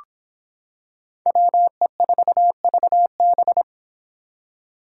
Morse code, heard as a keyed tone, about 26 wpm, 700 Hz. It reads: WE4VB